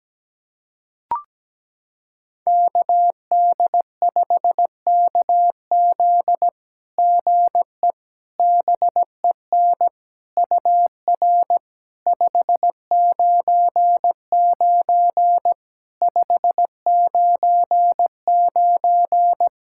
Morse code, heard as a keyed tone, about 17 wpm, 700 Hz. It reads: KD5KZ GE BEN UR 599 599
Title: SOTA QSO Practice 17wpm